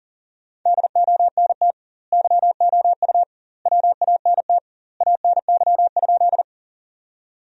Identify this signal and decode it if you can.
Morse code, heard as a keyed tone, about 40 wpm, 700 Hz. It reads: DONT YOU WANT ANY?
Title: 2x Word-Spacing - Sentences from Top 300 Words - 40wpm